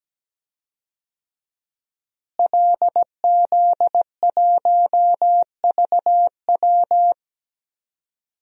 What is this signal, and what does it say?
Morse code, heard as a keyed tone, about 17 wpm, 700 Hz. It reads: LZ1VW
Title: DXCC-100 Call Signs - 5 Characters - Repeated 3x - 17wpm